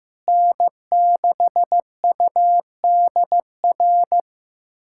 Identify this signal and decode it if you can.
Morse code, heard as a keyed tone, about 15 wpm, 700 Hz. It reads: N6UDR
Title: USA Callsign DE Callsign - 5 Characters - 15wpm